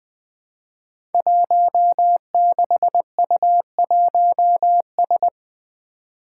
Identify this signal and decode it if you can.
Morse code, heard as a keyed tone, about 21 wpm, 700 Hz. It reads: 16U1S
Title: Five Challenging Letter-Number Combinations 20wpm